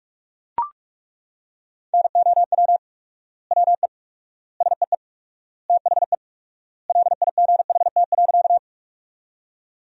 Morse code, heard as a keyed tone, about 41 wpm, 700 Hz. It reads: NOW WE SEE THE LIGHT.
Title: 4x Word-Spacing - Sentences from Top 200 Words - 45wpm